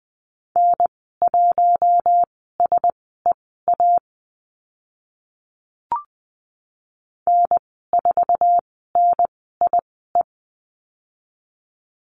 Morse code, heard as a keyed tone, about 20 wpm, spaced as for 14 wpm, 700 Hz. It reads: N1SEA N4NIE
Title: 069 Lesson 12g - Letter d - Call Signs 20wpm